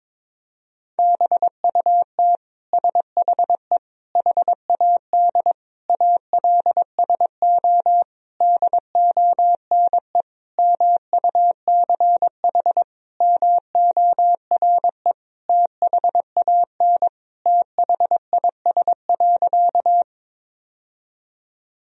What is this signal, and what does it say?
Morse code, heard as a keyed tone, about 22 wpm, 700 Hz. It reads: BUT SHE HAD ALSO DONE MUCH MORE THAN THIS.
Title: Sentences from Top 1000 Words - Review Part 1 - 22wpm